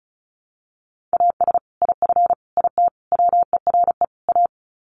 Morse code, heard as a keyed tone, about 34 wpm, 700 Hz. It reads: AS IF IT WERE A
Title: Sets of 5 Words 35wpm